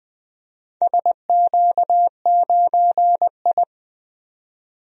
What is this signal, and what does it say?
Morse code, heard as a keyed tone, about 20 wpm, 700 Hz. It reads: SQ9I
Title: DXCC-100 Call Signs - 4 Characters - 20wpm